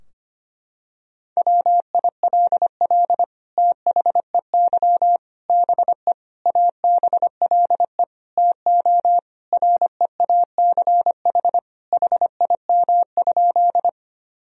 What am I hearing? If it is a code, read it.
Morse code, heard as a keyed tone, about 25 wpm, 700 Hz. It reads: WILL THEY BE ABLE TO REACH HIM?
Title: Sentences from Top 1000 Words - Review Part 2 - 25wpm